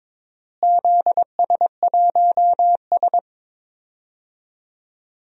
Morse code, heard as a keyed tone, about 22 wpm, 700 Hz. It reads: ZS1S
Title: DXCC-100 Call Signs - 4 Characters - Repeated 2x - 22wpm